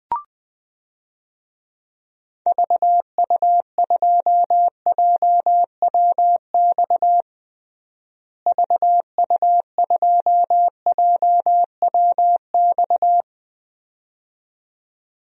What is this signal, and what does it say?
Morse code, heard as a keyed tone, about 21 wpm, 700 Hz. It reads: VU2JWX VU2JWX
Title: DXCC-100 Call Signs - 6 Characters - Repeated 2x - 20wpm